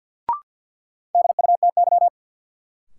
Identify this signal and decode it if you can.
Morse code, heard as a keyed tone, about 53 wpm, 700 Hz. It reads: DUTY
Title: Sentences from Top 1500 Words - New Word Drill - 50wpm